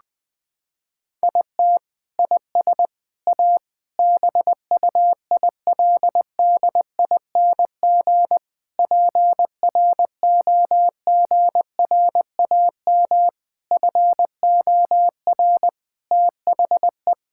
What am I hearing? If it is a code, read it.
Morse code, heard as a keyed tone, about 21 wpm, 700 Hz. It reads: IT IS A BUILDING PROGRAM FOR THE
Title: Sentences from Top 1000 Words - Review Part 5 - 20wpm